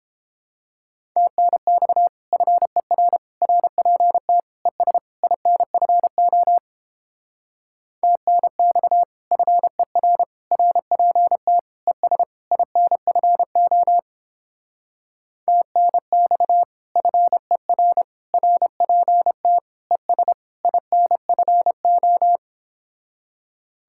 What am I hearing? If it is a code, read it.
Morse code, heard as a keyed tone, about 29 wpm, 700 Hz. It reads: TNX FER RPT ES INFO TNX FER RPT ES INFO TNX FER RPT ES INFO